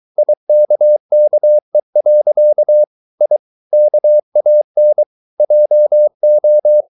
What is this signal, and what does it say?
Morse code, heard as a keyed tone, about 23 wpm, 600 Hz. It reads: IKKE. I KAN JO
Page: Ferie 23 wpm